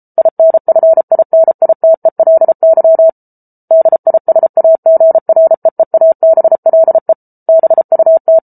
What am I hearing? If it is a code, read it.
Morse code, heard as a keyed tone, about 34 wpm, 650 Hz. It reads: INFINITELY DISAGREEABLE BUT